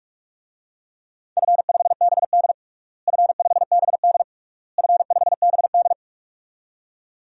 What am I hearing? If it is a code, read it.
Morse code, heard as a keyed tone, about 44 wpm, 700 Hz. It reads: F5BD F5BD F5BD
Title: DXCC-100 Call Signs - 4 Characters - Repeated 3x - 45wpm